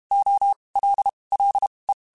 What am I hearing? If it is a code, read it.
Morse code, heard as a keyed tone, about 32 wpm, spaced as for 23 wpm, 800 Hz. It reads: OLLE